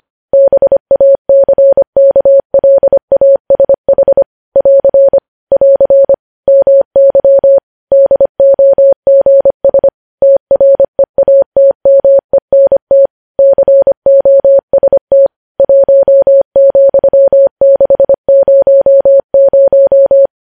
Morse code, heard as a keyed tone, about 25 wpm, 550 Hz. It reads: BACKLASH <AR> <AR> MY DOGS TREATMENT COST 1,600
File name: podcast / News headlines-1x-25x25